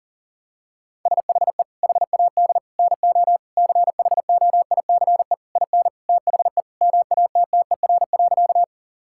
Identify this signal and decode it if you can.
Morse code, heard as a keyed tone, about 40 wpm, 700 Hz. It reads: SHE HAD NO CHOICE IN THE MATTER.
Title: Sentences from Top 1000 Words - Review Part 6 - 40wpm